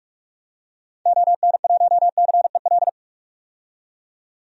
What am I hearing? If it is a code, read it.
Morse code, heard as a keyed tone, about 46 wpm, 700 Hz. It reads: ON1CEL
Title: DXCC-100 Call Signs - 6 Characters - 45wpm